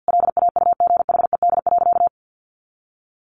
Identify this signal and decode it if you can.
Morse code, heard as a keyed tone, about 50 wpm, 700 Hz. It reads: LAUGHED.
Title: A Sound Of Thunder 50wpm